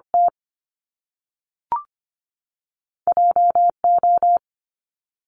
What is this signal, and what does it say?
Morse code, heard as a keyed tone, about 25 wpm, 700 Hz. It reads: T J O
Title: Top Two Letter Combinations in Words 25wpm